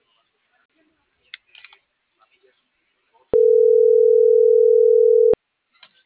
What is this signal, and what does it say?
Ringback tone (North America), 440 + 480 Hz.